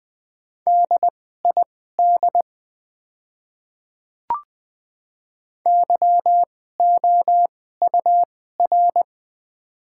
Morse code, heard as a keyed tone, about 20 wpm, spaced as for 14 wpm, 700 Hz. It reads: DID YOUR
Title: 2x Character Spacing - Top 100 Words - 20wpm